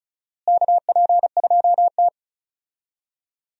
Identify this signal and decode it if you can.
Morse code, heard as a keyed tone, about 34 wpm, 700 Hz. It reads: KP2T
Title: USA Call Signs - 4 Characters - Repeated 3x - 35wpm